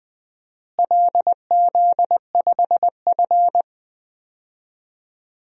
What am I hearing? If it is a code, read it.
Morse code, heard as a keyed tone, about 20 wpm, 700 Hz. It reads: LZ5F
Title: DXCC-100 Call Signs - 4 Characters - Repeated 3x - 20wpm